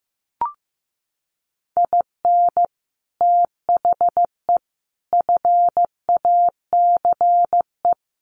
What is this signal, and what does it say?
Morse code, heard as a keyed tone, about 14 wpm, 700 Hz. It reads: IN THE FACE
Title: Sets of 4 Words 15wpm